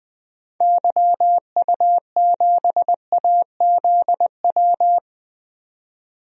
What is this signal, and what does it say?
Morse code, heard as a keyed tone, about 21 wpm, 700 Hz. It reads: YU7AZW
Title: DXCC-100 Call Signs - 6 Characters - 20wpm